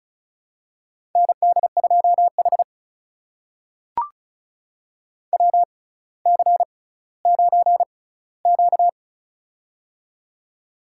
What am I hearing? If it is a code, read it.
Morse code, heard as a keyed tone, about 36 wpm, spaced as for 18 wpm, 700 Hz. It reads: ND2H WC9Q
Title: USA Call Signs - 4 Characters - 6x Character Spacing - 35wpm